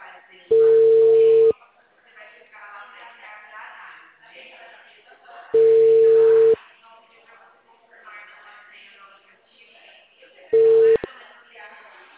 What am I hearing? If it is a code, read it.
Ringback tone (China, 450 Hz), with speech.
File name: out- -2044- - - .8262